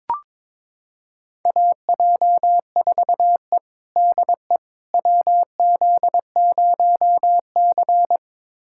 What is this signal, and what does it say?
Morse code, heard as a keyed tone, about 22 wpm, 700 Hz. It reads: AJ4E DE WZ0C
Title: USA Callsign DE Callsign - 4 Characters - 22wpm